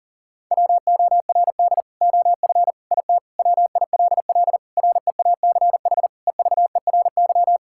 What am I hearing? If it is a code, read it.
Morse code, heard as a keyed tone, about 37 wpm, 700 Hz. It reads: WORD OF IT WILL REACH EVERY
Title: Sentences from Top 1600 Words 40wpm